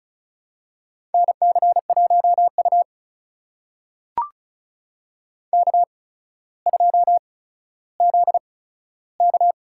Morse code, heard as a keyed tone, about 36 wpm, spaced as for 15 wpm, 700 Hz. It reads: NC1U K2ZK
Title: USA Call Signs - 4 Characters - 8x Character Spacing - 35wpm